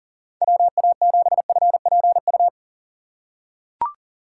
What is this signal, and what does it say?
Morse code, heard as a keyed tone, about 40 wpm, 700 Hz. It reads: WA7FPU
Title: USA Call Signs - 6 Characters - 2x Character Spacing - 40wpm